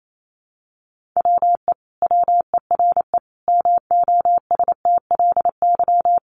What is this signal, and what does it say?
Morse code, heard as a keyed tone, about 28 wpm, 700 Hz. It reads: WE WERE MOSTLY
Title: Sentences from Top 1300 Words 28wpm